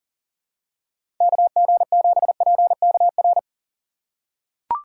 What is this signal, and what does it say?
Morse code, heard as a keyed tone, about 40 wpm, 700 Hz. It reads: KG7PKR
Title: USA Call Signs - 6 Characters - 4x Character Spacing - 40wpm